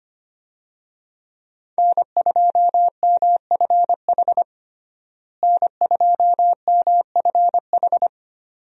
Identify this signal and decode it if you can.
Morse code, heard as a keyed tone, about 25 wpm, 700 Hz. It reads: N2MFH N2MFH
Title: USA Call Signs - 5 Characters - Repeated 2x - 25wpm